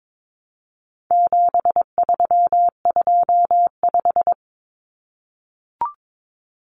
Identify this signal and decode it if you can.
Morse code, heard as a keyed tone, about 22 wpm, 700 Hz. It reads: 7325
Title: Numbers - 4-Digit 22wpm